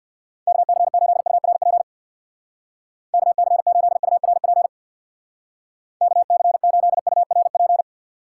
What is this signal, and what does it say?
Morse code, heard as a keyed tone, about 65 wpm, 700 Hz. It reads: KC8URP KC8URP KC8URP